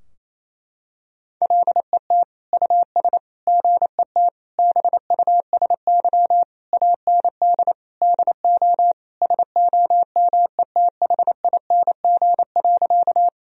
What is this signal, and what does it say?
Morse code, heard as a keyed tone, about 28 wpm, 690 Hz. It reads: LET US GET BUSY AND DO SOMETHING.
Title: Sentences from Top 1000 Words - Review Part 3 - 28wpm